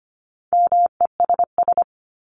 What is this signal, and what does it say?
Morse code, heard as a keyed tone, about 25 wpm, 700 Hz. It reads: MESS
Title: Top 1751-2000 Words 25wpm